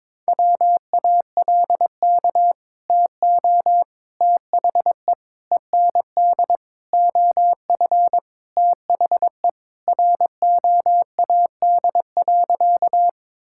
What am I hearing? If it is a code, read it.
Morse code, heard as a keyed tone, about 22 wpm, 700 Hz. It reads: WALK TO THE END OF THE ROAD.
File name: practice / 22wpm / Sentences with 4-Word Phrases - 22wpm